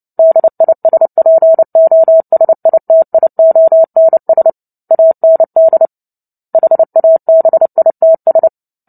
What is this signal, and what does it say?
Morse code, heard as a keyed tone, about 30 wpm, 650 Hz. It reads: DISPOSITIONS AND HABITS
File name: Book Two / 03-Chapter 3 - 29 WPM